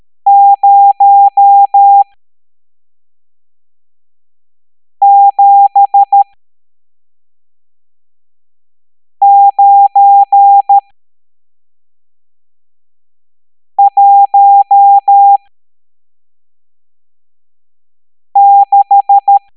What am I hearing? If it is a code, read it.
Morse code, heard as a keyed tone, about 13 wpm, spaced as for 3 wpm, 800 Hz. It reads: 07916
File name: hamdata / L12-EX02-CHIFFRES